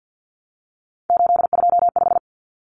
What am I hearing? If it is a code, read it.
Morse code, heard as a keyed tone, about 50 wpm, 700 Hz. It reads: 825
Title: Common Cut Numbers - 3-Digit 50wpm